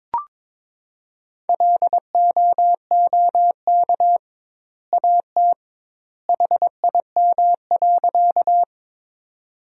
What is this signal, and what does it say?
Morse code, heard as a keyed tone, about 22 wpm, 700 Hz. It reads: LOOK AT HIM.
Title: 2x Word-Spacing - Sentences from Top 100 Words - 22wpm